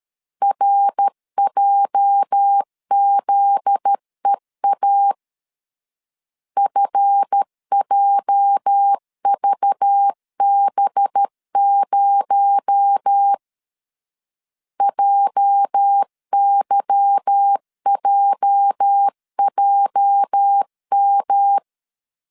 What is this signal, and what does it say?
Morse code, heard as a keyed tone, about 13 wpm, 800 Hz. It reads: RJZEA FJVB0 JYJJM
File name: LEZIONI / 8 YJ45 / 8 YJ 12wpm 2235